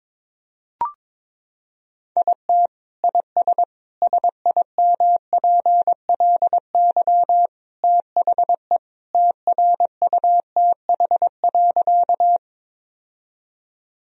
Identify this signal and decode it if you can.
Morse code, heard as a keyed tone, about 21 wpm, 700 Hz. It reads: IT IS SIMPLY THE TRUTH.
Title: Sentences from Top 1000 Words - Review Part 4 - 22wpm